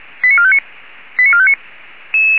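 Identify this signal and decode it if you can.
Five-tone selective call (ZVEI), heard as quick runs of five tones, tones of about 70 ms. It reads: 89459 89459
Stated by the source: ZVEI five-tone selective call 89231, received by SDR.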